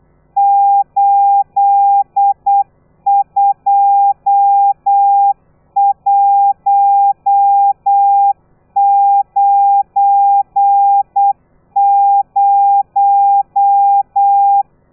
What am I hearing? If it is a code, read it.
Morse code, heard as a keyed tone, about 8 wpm, 800 Hz. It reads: 82190
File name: telegrafia / i2tfj / Lez12 35